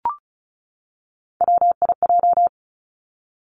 Morse code, heard as a keyed tone, about 36 wpm, 700 Hz. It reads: WIJ
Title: Three Challenging Letter-Number Combinations 35wpm